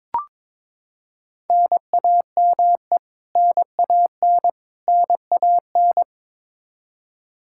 Morse code, heard as a keyed tone, about 21 wpm, 700 Hz. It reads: NAME NAN NAN
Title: Practice QSO - Exchange 1 - Beginning - 22wpm